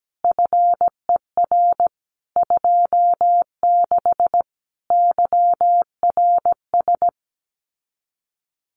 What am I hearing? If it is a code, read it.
Morse code, heard as a keyed tone, about 17 wpm, 700 Hz. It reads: FER 26 YRS
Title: Common QSO Phrases Encore 17wpm